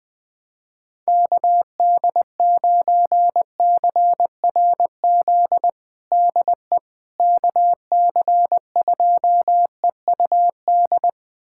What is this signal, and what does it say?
Morse code, heard as a keyed tone, about 20 wpm, 700 Hz. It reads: KD9CRZ DE KC2EUD
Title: USA Callsign DE Callsign 20wpm